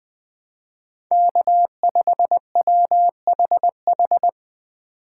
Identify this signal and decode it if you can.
Morse code, heard as a keyed tone, about 20 wpm, 700 Hz. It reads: K5WHH